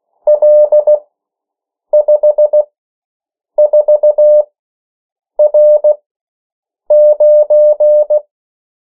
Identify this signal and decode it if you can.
Morse code, heard as a keyed tone, about 16 wpm, spaced as for 7 wpm, 600 Hz. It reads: L54R9
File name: morse mp3 / 14 urok 13